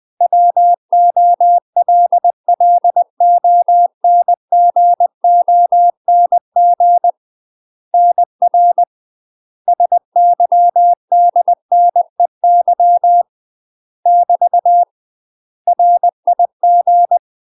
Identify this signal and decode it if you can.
Morse code, heard as a keyed tone, about 20 wpm, 700 Hz. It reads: WOLLONGONG NR SYDNEY <BT> RIG
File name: UK MASTER QSOs TRAINING / 20WPM / UK MASTER QSOs 0022